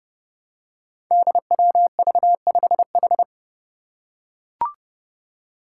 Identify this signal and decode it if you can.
Morse code, heard as a keyed tone, about 32 wpm, 700 Hz. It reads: DWV5H
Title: Five Challenging Letter-Number Combinations 30wpm